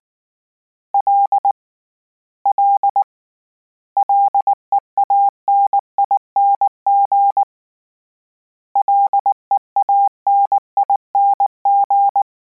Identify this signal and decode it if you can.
Morse code, heard as a keyed tone, about 19 wpm, 800 Hz. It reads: L L LEANING LEANING